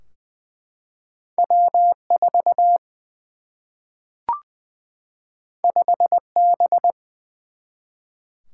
Morse code, heard as a keyed tone, about 20 wpm, 700 Hz. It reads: W 4 5 B
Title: Single-Letter Single-Number 20wpm